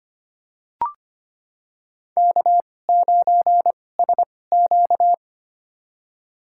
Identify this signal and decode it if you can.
Morse code, heard as a keyed tone, about 25 wpm, spaced as for 18 wpm, 700 Hz. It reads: K9SQ
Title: USA Call Signs - 4 Characters - 2x Character Spacing - 25wpm